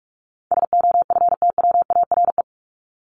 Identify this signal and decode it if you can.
Morse code, heard as a keyed tone, about 44 wpm, 700 Hz. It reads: SOFTWARE
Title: Top 501-750 Words 45wpm